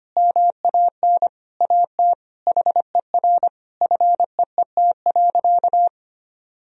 Morse code, heard as a keyed tone, about 25 wpm, 700 Hz. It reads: MAN AT HER FEET.
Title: Sentences from Top 1000 Words - Review Part 1 - 25wpm